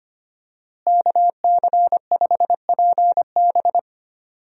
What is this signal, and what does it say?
Morse code, heard as a keyed tone, about 25 wpm, 700 Hz. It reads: KC5PB
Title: USA Call Signs - 5 Characters - Repeated 3x - 25wpm